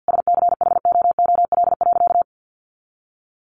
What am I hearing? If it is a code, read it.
Morse code, heard as a keyed tone, about 48 wpm, 700 Hz. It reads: SCHOOL.
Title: Easy Phrases 50wpm